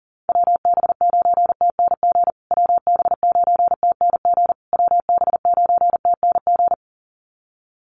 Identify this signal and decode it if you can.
Morse code, heard as a keyed tone, about 40 wpm, 700 Hz. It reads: WB9TNG WB9TNG WB9TNG
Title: USA Call Signs Repeated 3x 40wpm